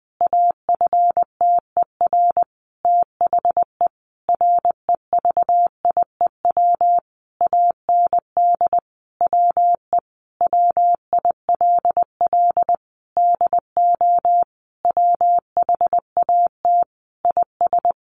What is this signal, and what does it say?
Morse code, heard as a keyed tone, about 20 wpm, 700 Hz. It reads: AFTER THE REVIEW AND WE WILL DO WHAT IS
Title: Sentences from Top 900 Words 20wpm